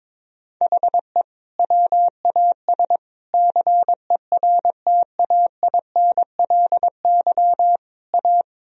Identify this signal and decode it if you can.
Morse code, heard as a keyed tone, about 22 wpm, 700 Hz. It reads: HE WAS CERTAINLY A
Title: Sentences from Top 1000 Words - Review Part 1 - 22wpm